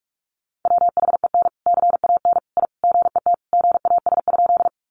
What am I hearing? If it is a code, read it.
Morse code, heard as a keyed tone, about 44 wpm, 700 Hz. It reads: WHEN CAN I GET GAS?
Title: Sentences from Top 1100 Words 45wpm